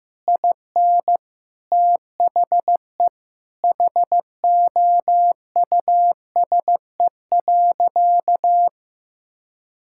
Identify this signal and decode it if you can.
Morse code, heard as a keyed tone, about 16 wpm, 700 Hz. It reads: IN THE HOUSE.
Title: Sentences from Top 1600 Words 15wpm